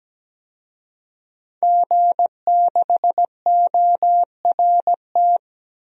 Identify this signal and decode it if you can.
Morse code, heard as a keyed tone, about 17 wpm, 700 Hz. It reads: G6ORT
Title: DXCC-100 Call Signs - 5 Characters - 17wpm